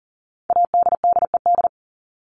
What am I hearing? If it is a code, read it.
Morse code, heard as a keyed tone, about 40 wpm, 700 Hz. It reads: ADDED
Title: ED Suffix - Top 100 Words with Base Words 40wpm